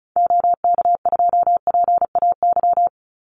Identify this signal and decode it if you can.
Morse code, heard as a keyed tone, about 36 wpm, 700 Hz. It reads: OK2PAY
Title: Call Signs - 35wpm